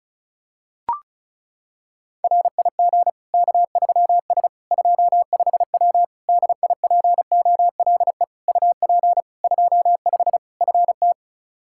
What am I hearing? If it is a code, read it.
Morse code, heard as a keyed tone, about 34 wpm, 700 Hz. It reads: RIG K3S 25W DIPOLE UP 25 FT
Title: K1USN SST QSO Practice 35wpm